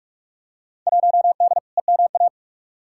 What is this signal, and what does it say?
Morse code, heard as a keyed tone, about 46 wpm, 700 Hz. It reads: 1D EMA
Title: Field Day Exchanges - 45wpm